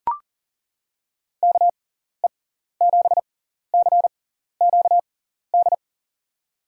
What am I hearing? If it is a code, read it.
Morse code, heard as a keyed tone, about 40 wpm, spaced as for 14 wpm, 700 Hz. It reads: KE7CQD
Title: USA Call Signs - 6 Characters - 6x Character Spacing - 40wpm